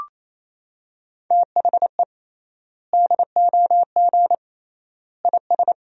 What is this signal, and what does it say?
Morse code, heard as a keyed tone, about 28 wpm, 700 Hz. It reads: THE DOG IS
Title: 3x Word-Spacing - Sentences from Top 400 Words - 28wpm